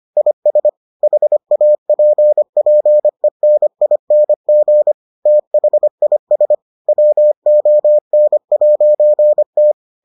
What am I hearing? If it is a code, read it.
Morse code, heard as a keyed tone, about 25 wpm, 600 Hz. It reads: IS HAPPENING THIS WON'T